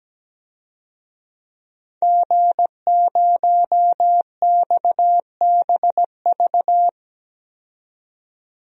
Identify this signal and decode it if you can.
Morse code, heard as a keyed tone, about 17 wpm, 700 Hz. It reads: G0XBV
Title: DXCC-100 Call Signs - 5 Characters - 17wpm